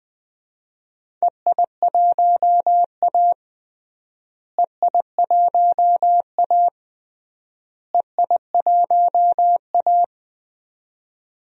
Morse code, heard as a keyed tone, about 21 wpm, 690 Hz. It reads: EI1A EI1A EI1A
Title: DXCC-100 Call Signs - 4 Characters - Repeated 3x - 20wpm